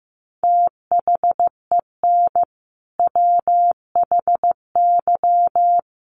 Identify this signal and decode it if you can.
Morse code, heard as a keyed tone, about 15 wpm, 700 Hz. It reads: THEN WHY
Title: Sentences from Top 800 Words 15wpm